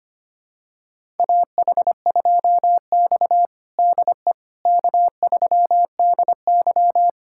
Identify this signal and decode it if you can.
Morse code, heard as a keyed tone, about 25 wpm, 700 Hz. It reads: AH2X DE K3DY
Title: USA Callsign DE Callsign - 4 Characters - 25wpm